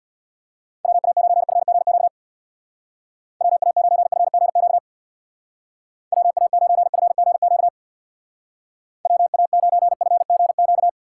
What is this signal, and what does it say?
Morse code, heard as a keyed tone, about 62 wpm, 700 Hz. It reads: WA9FGQ WA9FGQ WA9FGQ WA9FGQ
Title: Speed Racing - USA 2x3 Call Signs 50wpm